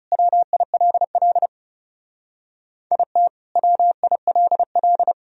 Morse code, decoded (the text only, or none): WILL IT WILL